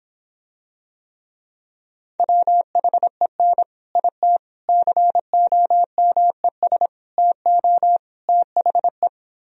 Morse code, heard as a keyed tone, about 26 wpm, 700 Hz. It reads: WHEN IT COMES TO THE